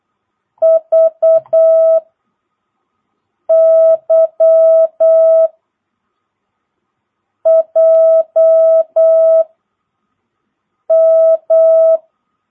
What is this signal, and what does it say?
Morse code, heard as a keyed tone, about 8 wpm, 660 Hz. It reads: V Y J M
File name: jh8cbh / 2022cw